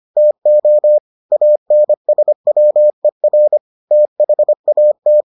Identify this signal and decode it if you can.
Morse code, heard as a keyed tone, about 25 wpm, 600 Hz. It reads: TO ANSWER THAT